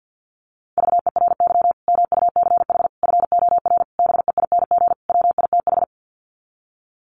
Morse code, heard as a keyed tone, about 50 wpm, 700 Hz. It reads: VERY MUCH FOR BEING WITH
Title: Sets of 5 Words 50wpm